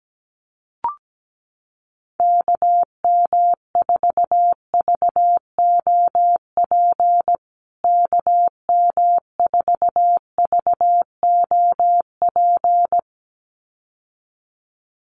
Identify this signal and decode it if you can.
Morse code, heard as a keyed tone, about 17 wpm, 700 Hz. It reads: KM4VOP KM4VOP
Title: USA Call Signs Repeated 2x 17wpm